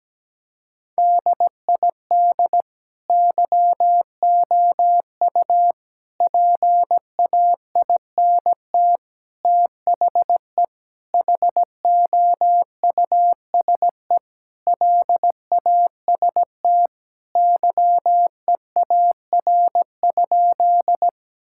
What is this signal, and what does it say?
Morse code, heard as a keyed tone, about 17 wpm, 700 Hz. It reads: DID YOU PAINT THE HOUSE LAST YEAR?
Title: Sentences from Top 1000 Words - Review Part 6 - 17wpm